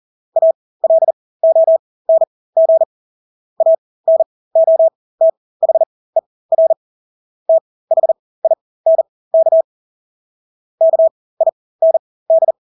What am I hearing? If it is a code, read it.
Morse code, heard as a keyed tone, about 40 wpm, spaced as for 20 wpm, 650 Hz. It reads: ALONG ANOTHER THINK KIND
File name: Engelske ord 40-20 wpm